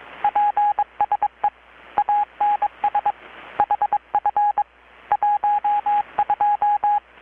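Morse code, heard as a keyed tone, about 21 wpm, 830 Hz, with radio noise. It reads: PSE ANS HF 12
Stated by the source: Morse from KPH's Night of Nights XX maritime radiotelegraphy event, received in Chiapas, Mexico about 3,800 km away on a Kenwood R-600 with a grounded 10 m long-wire antenna.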